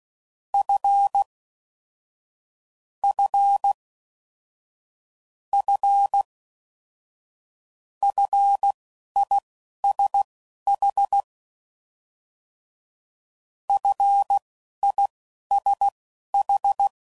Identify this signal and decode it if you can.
Morse code, heard as a keyed tone, about 16 wpm, spaced as for 11 wpm, 800 Hz. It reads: F F F FISH FISH
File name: Prefixes / FLs & THs / F Prefixes